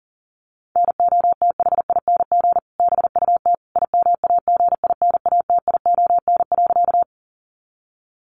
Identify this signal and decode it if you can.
Morse code, heard as a keyed tone, about 40 wpm, 700 Hz. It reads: NOTHING BUT IMAGINATION.
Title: Sentences from Top 1300 Words 40wpm